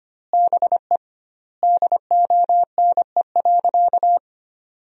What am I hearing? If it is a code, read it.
Morse code, heard as a keyed tone, about 25 wpm, 700 Hz. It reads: BE DONE.
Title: 2x Word-Spacing - Sentences from Top 500 Words - 25wpm